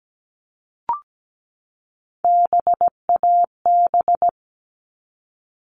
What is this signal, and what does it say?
Morse code, heard as a keyed tone, about 17 wpm, 700 Hz. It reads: BAB
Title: All Cut Numbers - 3-Digit 17wpm